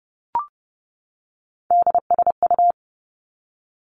Morse code, heard as a keyed tone, about 30 wpm, 700 Hz. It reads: DSU
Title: Three Challenging Letter-Number Combinations 30wpm